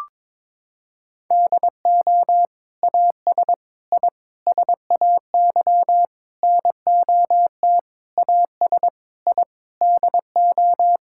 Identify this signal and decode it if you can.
Morse code, heard as a keyed tone, about 22 wpm, 700 Hz. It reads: DO AS I SAY NOT AS I DO
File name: practice / 22wpm / Sets of 8 Words - 22wpm